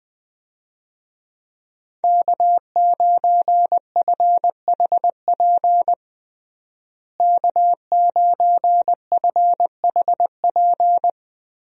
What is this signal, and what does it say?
Morse code, heard as a keyed tone, about 20 wpm, 700 Hz. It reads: K9FHP K9FHP
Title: USA Call Signs - 5 Characters - Repeated 3x - 20wpm